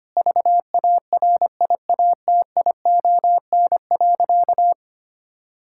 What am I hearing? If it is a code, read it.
Morse code, heard as a keyed tone, about 25 wpm, 700 Hz. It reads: VARIATION.
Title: Sentences from Top 1600 Words 25wpm